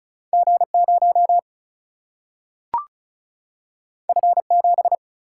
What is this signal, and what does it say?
Morse code, heard as a keyed tone, about 35 wpm, 700 Hz. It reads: G0 F7
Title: Single-Letter Single-Number 35wpm